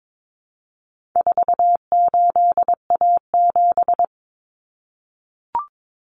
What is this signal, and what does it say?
Morse code, heard as a keyed tone, about 22 wpm, 700 Hz. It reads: 48A7
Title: Common Cut Numbers - 4-Digit 22wpm